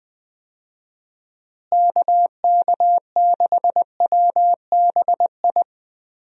Morse code, heard as a keyed tone, about 21 wpm, 700 Hz. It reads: KK6WBI
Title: USA Call Signs - 6 Characters - Repeated 3x - 20wpm